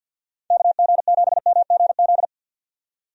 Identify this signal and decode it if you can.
Morse code, heard as a keyed tone, about 51 wpm, 700 Hz. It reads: KG7MGZ
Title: USA Call Signs - 6 Characters - 50wpm